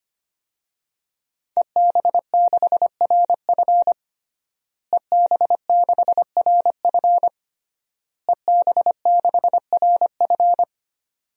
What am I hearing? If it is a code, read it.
Morse code, heard as a keyed tone, about 25 wpm, 700 Hz. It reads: EB6RF EB6RF EB6RF
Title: DXCC-100 Call Signs - 5 Characters - Repeated 3x - 25wpm